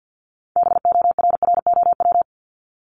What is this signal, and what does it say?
Morse code, heard as a keyed tone, about 50 wpm, 700 Hz. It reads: BORROW
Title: Top 1251-1500 Words 50wpm